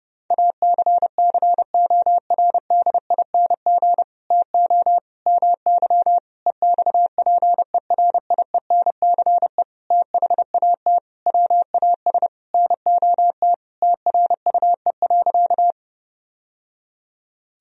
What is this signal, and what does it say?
Morse code, heard as a keyed tone, about 32 wpm, 700 Hz. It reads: ACCORDING TO MY EXPERIENCE THAT WAS NOT TRUE.
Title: Sentences from Top 1000 Words - Review Part 4 - 30wpm